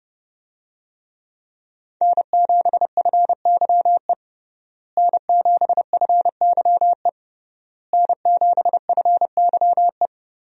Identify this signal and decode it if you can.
Morse code, heard as a keyed tone, about 31 wpm, 700 Hz. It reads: N7FYE N7FYE N7FYE
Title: USA Call Signs - 5 Characters - Repeated 3x - 30wpm